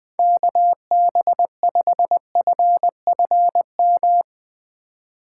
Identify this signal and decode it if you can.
Morse code, heard as a keyed tone, about 20 wpm, 700 Hz. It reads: KB5FFM
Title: USA Call Signs - 6 Characters - 4x Character Spacing - 20wpm